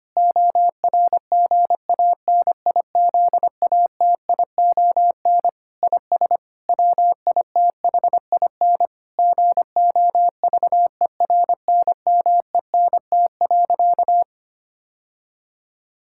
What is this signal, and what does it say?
Morse code, heard as a keyed tone, about 25 wpm, 700 Hz. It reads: ORGANIZATION IS WITHIN GOVERNMENT.
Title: Sentences from Top 1000 Words - Review Part 5 - 25wpm